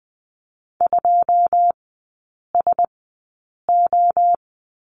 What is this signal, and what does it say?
Morse code, heard as a keyed tone, about 20 wpm, spaced as for 8 wpm, 690 Hz. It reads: 2SO
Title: 081 Lesson 14e - Number 2 - Three Characters 20wpm